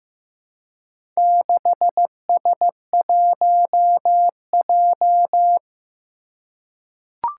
Morse code, heard as a keyed tone, about 15 wpm, 700 Hz. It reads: 6S1J
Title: Four Challenging Letter-Number Combinations 15wpm